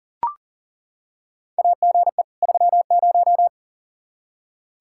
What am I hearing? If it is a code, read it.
Morse code, heard as a keyed tone, about 40 wpm, 700 Hz. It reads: AGE 30
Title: Practice QSO - Exchange 3 - More Info - 40wpm